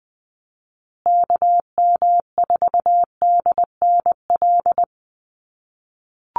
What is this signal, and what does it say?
Morse code, heard as a keyed tone, about 20 wpm, 700 Hz. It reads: KM4DNL
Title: USA Call Signs Repeated 3x 20wpm